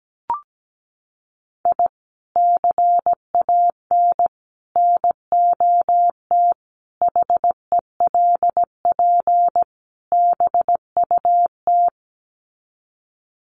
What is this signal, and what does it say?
Morse code, heard as a keyed tone, about 17 wpm, 700 Hz. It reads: I CAN NOT HELP BUT
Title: Sets of 5 Words Encore2 17wpm